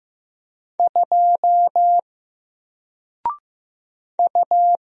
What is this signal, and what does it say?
Morse code, heard as a keyed tone, about 15 wpm, 700 Hz. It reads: 2 U
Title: Single Letter-Number Alt 15wpm